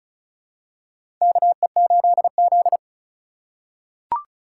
Morse code, heard as a keyed tone, about 34 wpm, 700 Hz. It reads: KE8Z
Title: USA Call Signs - 4 Characters - 2x Character Spacing - 35wpm